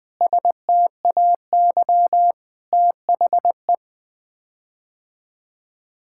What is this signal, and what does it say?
Morse code, heard as a keyed tone, about 20 wpm, 700 Hz. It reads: STAY THE
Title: Sets of 9 Words - 20wpm